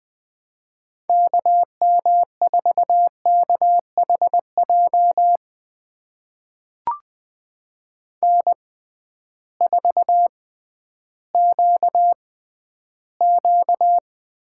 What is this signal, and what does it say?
Morse code, heard as a keyed tone, about 21 wpm, spaced as for 12 wpm, 700 Hz. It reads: KM4KHJ N4QQ
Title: USA Call Signs - 6x Character Spacing - 20wpm